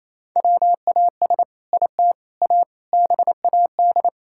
Morse code, heard as a keyed tone, about 28 wpm, 700 Hz. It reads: WAS IT A BAD
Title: Sentences from Top 1000 Words - Review Part 5 - 28wpm